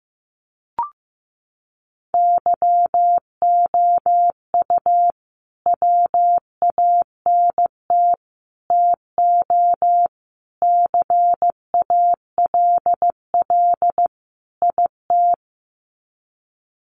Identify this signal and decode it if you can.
Morse code, heard as a keyed tone, about 15 wpm, 700 Hz. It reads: YOU WANT TO CALL IT
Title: Sets of 5 Words Encore2 15wpm